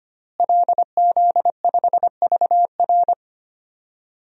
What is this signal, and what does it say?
Morse code, heard as a keyed tone, about 25 wpm, 700 Hz. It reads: LZ5VR
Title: DXCC-100 Call Signs - 5 Characters - Repeated 2x - 25wpm